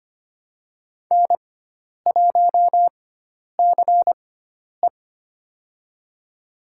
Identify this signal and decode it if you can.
Morse code, heard as a keyed tone, about 25 wpm, spaced as for 10 wpm, 700 Hz. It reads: N1CE
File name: Lesson 16g - Letter c - Call Signs 25wpm